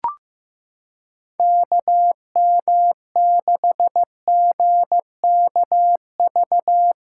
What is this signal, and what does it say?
Morse code, heard as a keyed tone, about 15 wpm, 700 Hz. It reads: KM6GKV
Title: USA Callsign DE Callsign 15wpm